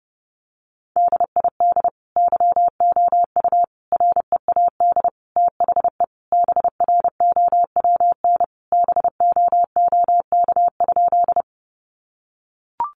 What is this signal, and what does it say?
Morse code, heard as a keyed tone, about 30 wpm, 700 Hz. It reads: DID YOU READ THE BROWN BOOK?
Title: Sentences from Top 1200 Words 30wpm